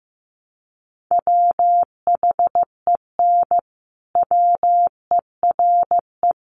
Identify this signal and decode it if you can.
Morse code, heard as a keyed tone, about 15 wpm, 700 Hz. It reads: WHEN WERE
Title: Sentences from Top 1300 Words 15wpm